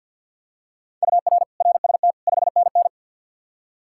Morse code, heard as a keyed tone, about 51 wpm, 700 Hz. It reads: UR RST 5NN
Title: K1USN SST QSO Practice 50wpm